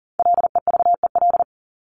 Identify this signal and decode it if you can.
Morse code, heard as a keyed tone, about 40 wpm, 700 Hz. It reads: LEVEL
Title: ES:S Suffix - Top 100 Words with Base Words 40wpm